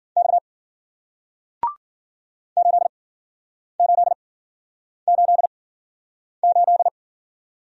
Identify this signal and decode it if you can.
Morse code, heard as a keyed tone, about 53 wpm, spaced as for 9 wpm, 700 Hz. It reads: X 8888